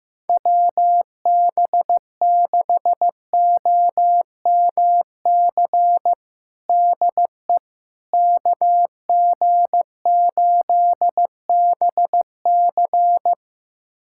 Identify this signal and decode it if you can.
Morse code, heard as a keyed tone, about 15 wpm, 700 Hz. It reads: WB6OMC DE KG8BC
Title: USA Callsign DE Callsign 15wpm